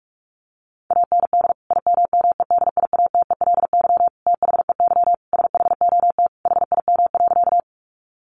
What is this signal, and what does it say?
Morse code, heard as a keyed tone, about 44 wpm, 700 Hz. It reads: AND IMMEDIATELY THEY SHOT HIM.
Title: Sentences from Top 1100 Words 45wpm